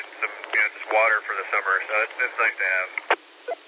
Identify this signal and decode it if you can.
Emergency Alert System (SAME) tones, heard as harsh buzzing bursts, with speech. It is heard from 0.3 to 3.0 s.